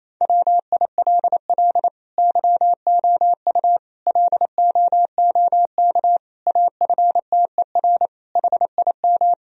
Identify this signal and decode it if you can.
Morse code, heard as a keyed tone, about 28 wpm, 700 Hz. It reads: WILL YOU LOOK AFTER HIM
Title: Sentences from Top 1000 Words - Review Part 1 - 28wpm